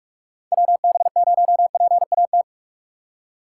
Morse code, heard as a keyed tone, about 46 wpm, 700 Hz. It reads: WB0PAT
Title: USA Call Signs - 6 Characters - 3x Character Spacing - 45wpm